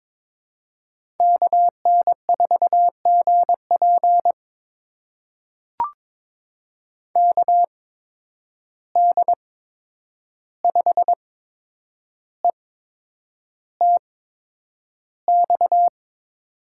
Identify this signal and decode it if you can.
Morse code, heard as a keyed tone, about 22 wpm, spaced as for 9 wpm, 700 Hz. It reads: KN4GP KD5ETX
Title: USA Call Signs - 8x Character Spacing - 22wpm